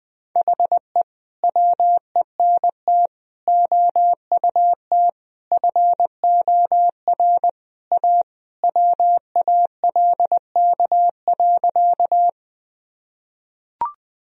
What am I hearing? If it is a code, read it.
Morse code, heard as a keyed tone, about 20 wpm, 700 Hz. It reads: HE WENT OUT FOR A WALK.
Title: Sentences from Top 1000 Words - Review Part 1 - 20wpm